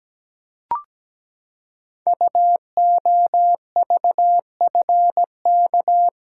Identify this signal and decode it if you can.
Morse code, heard as a keyed tone, about 17 wpm, 700 Hz. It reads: UOVFK
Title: Code Groups - 5 Letters - 17wpm